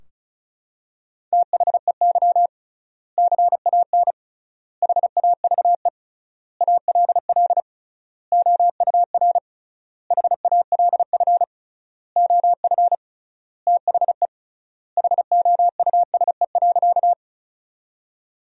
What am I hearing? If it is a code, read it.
Morse code, heard as a keyed tone, about 34 wpm, 700 Hz. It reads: THEY CAN HAVE ALL OUR HALF OF THE HOUSE.
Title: 3x Word-Spacing - Sentences from Top 400 Words - 35wpm